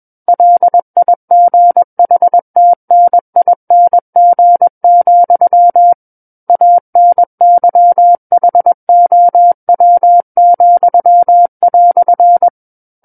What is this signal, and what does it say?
Morse code, heard as a keyed tone, about 21 wpm, 700 Hz. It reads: LIGHTNING, ANYHOW,"
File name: Book One / 12-Chapter 12 - 21 WPM